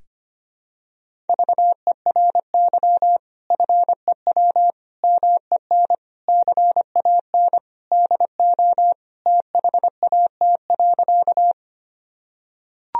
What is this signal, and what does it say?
Morse code, heard as a keyed tone, about 25 wpm, 700 Hz. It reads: VERY FEW MEN CAN DO THAT.
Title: Sentences from Top 1000 Words - Review Part 1 - 25wpm